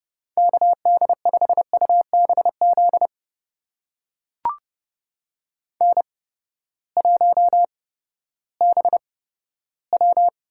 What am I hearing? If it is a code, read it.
Morse code, heard as a keyed tone, about 30 wpm, spaced as for 15 wpm, 700 Hz. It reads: KD5UBZ N1BW
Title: USA Call Signs - 8x Character Spacing - 30wpm